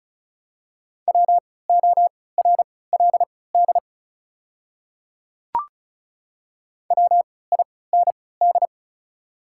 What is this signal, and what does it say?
Morse code, heard as a keyed tone, about 36 wpm, spaced as for 20 wpm, 690 Hz. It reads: WORLD WIND
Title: 3x Character Spacing - Top 500 Words - 35wpm